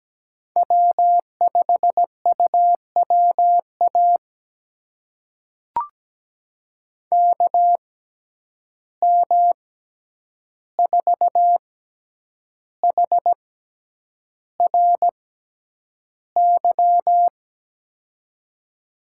Morse code, heard as a keyed tone, about 17 wpm, spaced as for 8 wpm, 700 Hz. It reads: W5UWA KM4HRY
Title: USA Call Signs - 6x Character Spacing - 17wpm